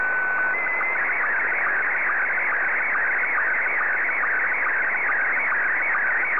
SSTV (slow-scan television), Scottie 1.
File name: sstv / S1-VK7ZGK